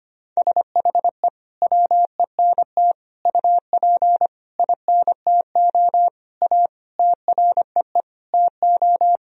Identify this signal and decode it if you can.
Morse code, heard as a keyed tone, about 25 wpm, 700 Hz. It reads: SHE WENT UP INTO A TREE TO
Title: Sentences from Top 1000 Words - Review Part 1 - 25wpm